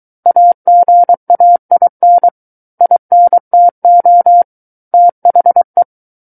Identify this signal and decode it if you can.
Morse code, heard as a keyed tone, about 23 wpm, 700 Hz. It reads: AGAIN INTO THE
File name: Book One / 14-Chapter 14 - 23 WPM